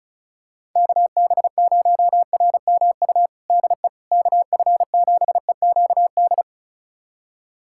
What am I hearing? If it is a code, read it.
Morse code, heard as a keyed tone, about 36 wpm, 700 Hz. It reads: KB0RMU DE KF7EQD
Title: USA Callsign DE Callsign - 6 Characters - 35wpm